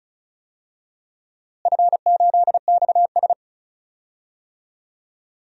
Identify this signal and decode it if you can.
Morse code, heard as a keyed tone, about 34 wpm, 700 Hz. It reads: F8XS
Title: DXCC-100 Call Signs - 4 Characters - Repeated 3x - 35wpm